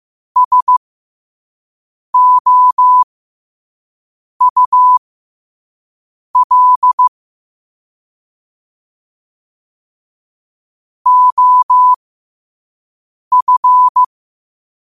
Morse code, heard as a keyed tone, about 15 wpm, spaced as for 5 wpm, 1000 Hz. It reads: SOUL OF
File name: qotdfiles / 5wpm